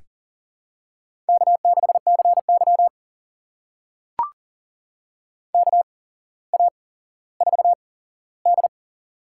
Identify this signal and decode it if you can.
Morse code, heard as a keyed tone, about 38 wpm, spaced as for 17 wpm, 700 Hz. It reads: K6CY KA4D
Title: USA Call Signs - 4 Characters - 8x Character Spacing - 40wpm